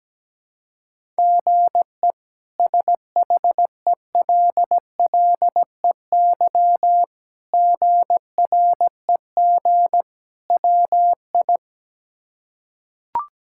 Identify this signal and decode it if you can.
Morse code, heard as a keyed tone, about 17 wpm, 700 Hz. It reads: GE SHELLEY GREG WI
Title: K1USN SST QSO Practice - 17wpm